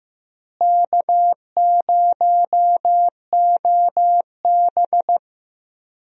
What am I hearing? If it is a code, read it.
Morse code, heard as a keyed tone, about 16 wpm, 700 Hz. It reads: K0OB
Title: USA Call Signs - 4 Characters - Repeated 3x - 15wpm